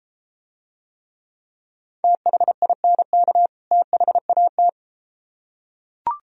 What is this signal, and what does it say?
Morse code, heard as a keyed tone, about 33 wpm, 700 Hz. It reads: THINK THAT